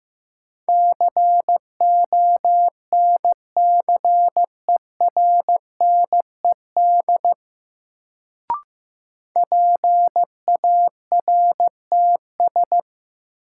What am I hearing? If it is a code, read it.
Morse code, heard as a keyed tone, about 15 wpm, 700 Hz. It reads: CONCERNED PARTS
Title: Sentences from Top 1000 Words - Review Part 6 - New Word Drill - 15wpm